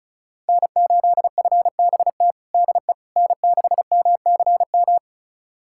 Morse code, heard as a keyed tone, about 36 wpm, 700 Hz. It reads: N8FBT DE N6MCM
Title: USA Callsign DE Callsign - 5 Characters - 35wpm